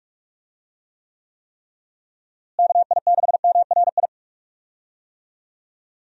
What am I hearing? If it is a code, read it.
Morse code, heard as a keyed tone, about 46 wpm, 700 Hz. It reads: KI6MRI
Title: USA Call Signs - 6 Characters - Repeated 3x - 45wpm